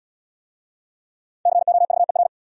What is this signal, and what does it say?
Morse code, heard as a keyed tone, about 70 wpm, 700 Hz. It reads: COVER